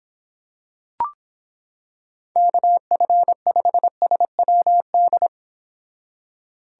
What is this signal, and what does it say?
Morse code, heard as a keyed tone, about 26 wpm, 700 Hz. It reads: KF5SWD